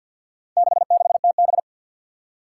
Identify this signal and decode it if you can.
Morse code, heard as a keyed tone, about 53 wpm, 700 Hz. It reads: 66TB
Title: Four Challenging Letter-Number Combinations 50wpm